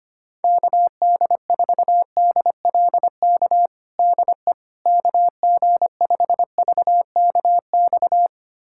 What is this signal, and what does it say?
Morse code, heard as a keyed tone, about 25 wpm, 700 Hz. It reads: KD4DLK DE KG5VKX
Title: USA Callsign DE Callsign - 6 Characters - 25wpm